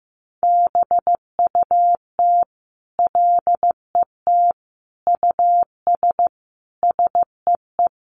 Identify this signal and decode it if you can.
Morse code, heard as a keyed tone, about 15 wpm, 700 Hz. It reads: BUT LET US SEE
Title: Sentences from Top 500 Words 15wpm